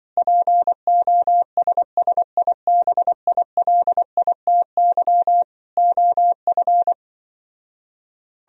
Morse code, heard as a keyed tone, about 25 wpm, 700 Hz. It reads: POSSIBILITY OF
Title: Speed-Racing - Sets of 3 Words 20wpm